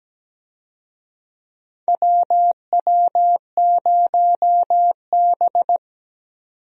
Morse code, heard as a keyed tone, about 17 wpm, 700 Hz. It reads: WW0B
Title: USA Call Signs - 4 Characters - 17wpm